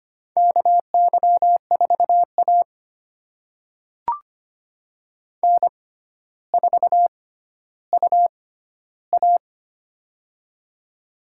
Morse code, heard as a keyed tone, about 25 wpm, spaced as for 13 wpm, 700 Hz. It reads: KY4A N4UA
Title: USA Call Signs - 4 Characters - 6x Character Spacing - 25wpm